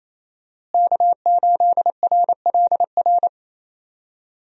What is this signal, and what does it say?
Morse code, heard as a keyed tone, about 28 wpm, 700 Hz. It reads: K8RLR
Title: USA Call Signs - 5 Characters - Repeated 3x - 28wpm